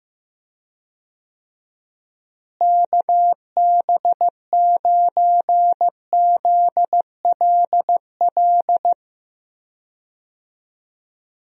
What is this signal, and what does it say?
Morse code, heard as a keyed tone, about 15 wpm, 700 Hz. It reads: KB9ZLL
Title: USA Call Signs - 6 Characters - Repeated 3x - 15wpm